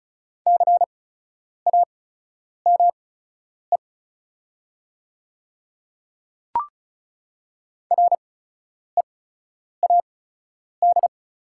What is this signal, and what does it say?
Morse code, heard as a keyed tone, about 36 wpm, spaced as for 10 wpm, 700 Hz. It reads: CAME READ
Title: 8x Character Spacing - Top 300 Words - 35wpm